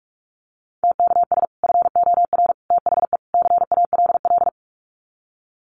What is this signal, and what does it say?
Morse code, heard as a keyed tone, about 44 wpm, 700 Hz. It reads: TKS FOR THE CALL
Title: Common QSO Phrases 45wpm